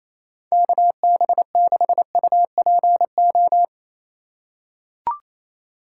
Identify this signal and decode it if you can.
Morse code, heard as a keyed tone, about 28 wpm, 700 Hz. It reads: KB6UPO
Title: USA Call Signs - 6 Characters - 3x Character Spacing - 28wpm